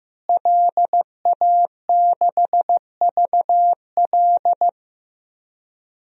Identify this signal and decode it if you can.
Morse code, heard as a keyed tone, about 15 wpm, 700 Hz. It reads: LA6VL
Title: DXCC-100 Call Signs - 5 Characters - Repeated 3x - 15wpm